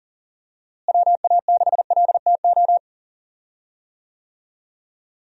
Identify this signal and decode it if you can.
Morse code, heard as a keyed tone, about 40 wpm, 700 Hz. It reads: WA6LTO
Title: USA Call Signs - 6 Characters - Repeated 2x - 40wpm